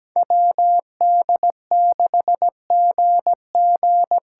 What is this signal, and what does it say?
Morse code, heard as a keyed tone, about 17 wpm, 700 Hz. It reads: WD6GG
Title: USA Call Signs - 6 Characters - 8x Character Spacing - 17wpm